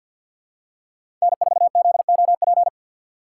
Morse code, heard as a keyed tone, about 49 wpm, 700 Hz. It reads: N4ZOP
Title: USA Call Signs - 5 Characters - 2x Character Spacing - 50wpm